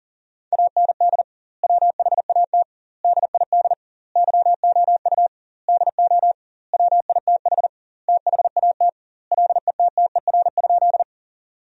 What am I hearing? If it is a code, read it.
Morse code, heard as a keyed tone, about 40 wpm, 700 Hz. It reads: AND WHAT DID YOU DO WITH THAT LETTER?
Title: 2x Word-Spacing - Sentences from Top 300 Words - 40wpm